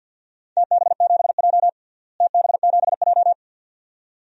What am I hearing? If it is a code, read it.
Morse code, heard as a keyed tone, about 52 wpm, 700 Hz. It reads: TB7J TB7J
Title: DXCC-100 Call Signs - 4 Characters - Repeated 2x - 50wpm